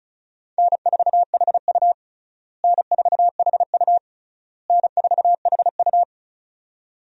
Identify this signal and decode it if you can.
Morse code, heard as a keyed tone, about 34 wpm, 700 Hz. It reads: N4HU N4HU N4HU
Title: USA Call Signs - 4 Characters - Repeated 3x - 35wpm